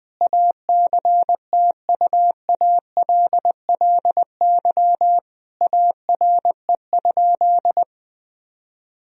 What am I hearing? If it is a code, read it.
Morse code, heard as a keyed tone, about 21 wpm, 700 Hz. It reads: ACTUALLY ARE?
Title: Sentences from Top 1000 Words - Review Part 4 - 20wpm